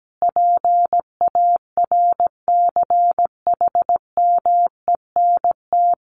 Morse code, heard as a keyed tone, about 17 wpm, 700 Hz. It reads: PARCHMENT
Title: A Sound Of Thunder 17wpm